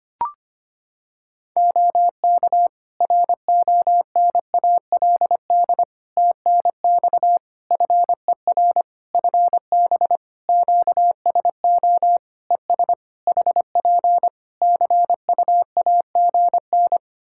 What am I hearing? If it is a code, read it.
Morse code, heard as a keyed tone, about 25 wpm, 700 Hz. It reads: OK RONALD TNX FER FB QSO ES HP CUAGN
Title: Practice QSO - Exchange 4 - Ending - 25wpm